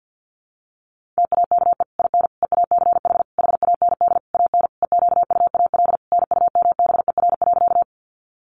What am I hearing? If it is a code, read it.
Morse code, heard as a keyed tone, about 50 wpm, 700 Hz. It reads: TAKE IN EACH HAND AN EQUAL NUMBER.
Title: Sentences from Top 1400 Words 50wpm